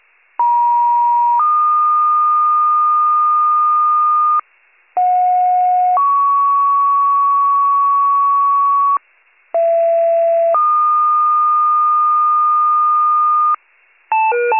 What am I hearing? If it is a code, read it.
Two-tone page, heard as a one-second tone, then a three-second tone, about 950 then 1220 Hz, about 730 then 1120 Hz, about 670 then 1220 Hz. It is heard from 0.4 to 13.5 s.